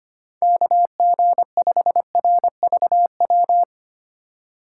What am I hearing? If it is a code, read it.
Morse code, heard as a keyed tone, about 25 wpm, 700 Hz. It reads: KG5RVW
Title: USA Call Signs - 6 Characters - 2x Character Spacing - 25wpm